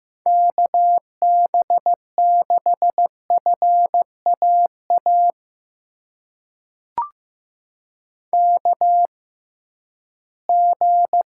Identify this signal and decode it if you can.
Morse code, heard as a keyed tone, about 15 wpm, spaced as for 11 wpm, 700 Hz. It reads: KB6FAA KG
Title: USA Call Signs - 6 Characters - 6x Character Spacing - 15wpm